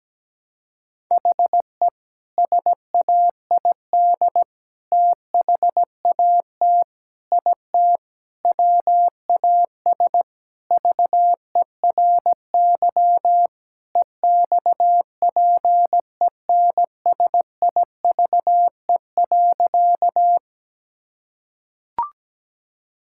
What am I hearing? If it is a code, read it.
Morse code, heard as a keyed tone, about 17 wpm, 700 Hz. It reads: HE SAID THAT IT WAS VERY EXPENSIVE.
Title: Sentences from Top 1000 Words - Review Part 6 - 17wpm